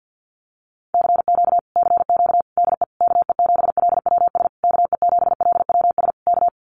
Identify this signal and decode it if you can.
Morse code, heard as a keyed tone, about 50 wpm, 700 Hz. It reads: CQ CQ DE KE7LWS KE7LWS K
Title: Common QSO Phrases Encore 50wpm